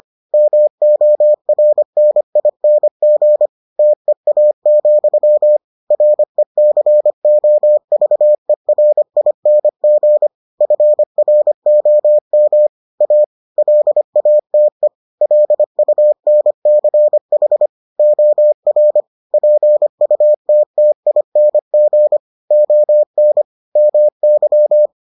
Morse code, heard as a keyed tone, about 25 wpm, 600 Hz. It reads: MORNING TEA, RECOVERING FROM A LATE LUNCH OR PUTTING ON MY